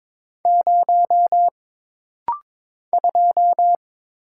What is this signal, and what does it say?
Morse code, heard as a keyed tone, about 22 wpm, 700 Hz. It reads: 0 2
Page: This rapid-fire timing is geared to push your ICR (Instant Character Recognition) skills.